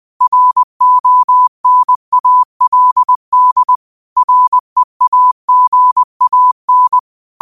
Morse code, heard as a keyed tone, about 20 wpm, 1000 Hz. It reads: RONALD REAGAN
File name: qotdfiles / 20wpm